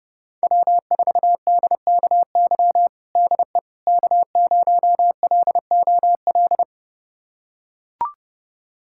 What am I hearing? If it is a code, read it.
Morse code, heard as a keyed tone, about 32 wpm, 700 Hz. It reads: W4DKY DE K0LOL
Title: USA Callsign DE Callsign - 5 Characters - 30wpm